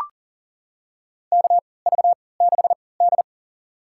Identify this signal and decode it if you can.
Morse code, heard as a keyed tone, about 40 wpm, spaced as for 23 wpm, 700 Hz. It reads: KV6D
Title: USA Call Signs - 3x Character Spacing - 40wpm